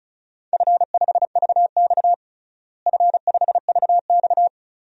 Morse code, heard as a keyed tone, about 34 wpm, 700 Hz. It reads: F5VX F5VX
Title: DXCC-100 Call Signs - 4 Characters - Repeated 2x - 35wpm